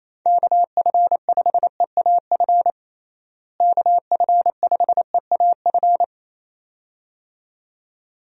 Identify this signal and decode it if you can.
Morse code, heard as a keyed tone, about 28 wpm, 700 Hz. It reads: KF5EAF KF5EAF
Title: USA Call Signs - 6 Characters - Repeated 2x - 28wpm